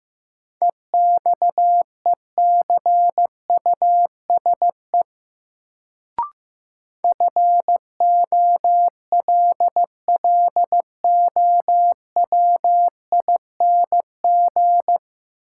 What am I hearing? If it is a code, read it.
Morse code, heard as a keyed tone, about 15 wpm, 700 Hz. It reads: EXECUSE FOLLOWING
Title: Sentences from Top 700 Words - New Word Drill - 15wpm